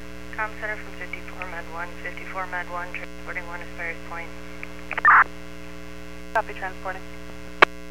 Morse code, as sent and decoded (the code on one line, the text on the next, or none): none